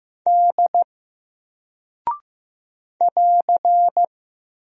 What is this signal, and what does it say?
Morse code, heard as a keyed tone, about 15 wpm, 700 Hz. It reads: D <AR>
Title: Single Character (Letter:Number:Punctuation:Prosign) Alt 15wpm